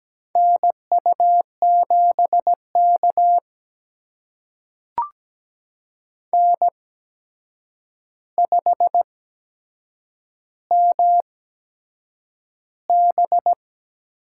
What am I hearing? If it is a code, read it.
Morse code, heard as a keyed tone, about 17 wpm, spaced as for 7 wpm, 700 Hz. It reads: NU7K N5MB
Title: USA Call Signs - 8x Character Spacing - 17wpm